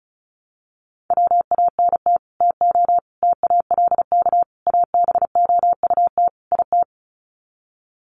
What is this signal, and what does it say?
Morse code, heard as a keyed tone, about 34 wpm, 700 Hz. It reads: WANT TO TALK ABOUT IT
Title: Sets of 5 Words Encore 35wpm